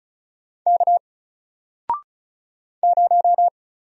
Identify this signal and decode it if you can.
Morse code, heard as a keyed tone, about 35 wpm, 700 Hz. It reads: K 0
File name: Single-Letter-Number-35wpm